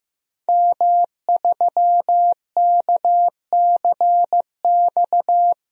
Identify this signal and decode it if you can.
Morse code, heard as a keyed tone, about 15 wpm, 700 Hz. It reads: M3KCX
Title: DXCC-100 Call Signs - 5 Characters - 15wpm